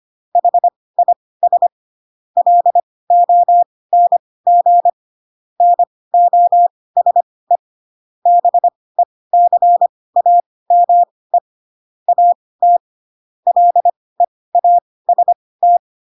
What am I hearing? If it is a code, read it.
Morse code, heard as a keyed tone, about 25 wpm, spaced as for 18 wpm, 700 Hz. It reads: HIS LONG NOSE BECAME AT LEAST